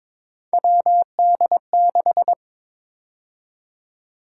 Morse code, heard as a keyed tone, about 22 wpm, 700 Hz. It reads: WD6
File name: practice / 22wpm / Three Letter-Number Combinations - 22wpm